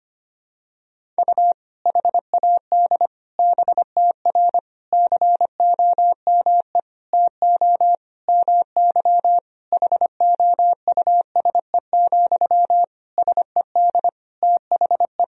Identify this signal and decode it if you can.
Morse code, heard as a keyed tone, about 25 wpm, 700 Hz. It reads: U HAD BTR COME TO MY HOUSE, SED THE
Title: Sentences from Top 400 Words and Abbreviations 25wpm